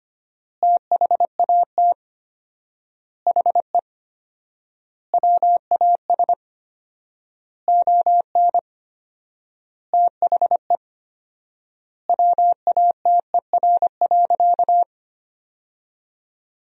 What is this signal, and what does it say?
Morse code, heard as a keyed tone, about 25 wpm, 700 Hz. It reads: THAT HE WAS ON THE WATER.
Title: 4x Word-Spacing - Sentences from Top 500 Words - 25wpm